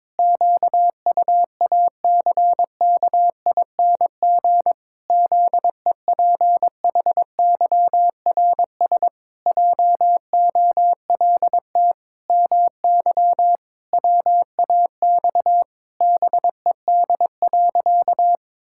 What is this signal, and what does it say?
Morse code, heard as a keyed tone, about 22 wpm, 700 Hz. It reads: QUACKING ZEPHYRS JOLT MY WAX BED.
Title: Pangram Challenge v2 - 22wpm